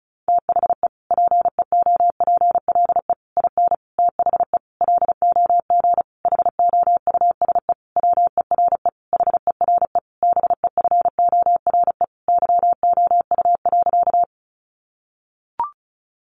Morse code, heard as a keyed tone, about 34 wpm, 700 Hz. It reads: THE PEOPLE IN THE LOG HOUSE WERE HERE BEFORE YOU.
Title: Sentences from Top 1400 Words 35wpm